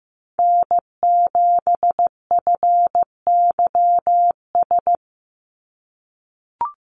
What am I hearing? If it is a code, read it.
Morse code, heard as a keyed tone, about 15 wpm, 700 Hz. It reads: N7FYS
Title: USA Call Signs 15wpm